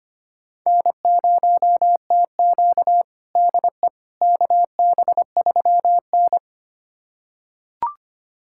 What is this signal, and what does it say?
Morse code, heard as a keyed tone, about 25 wpm, 700 Hz. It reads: N0TQ DE KB3N
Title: USA Callsign DE Callsign - 4 Characters - 25wpm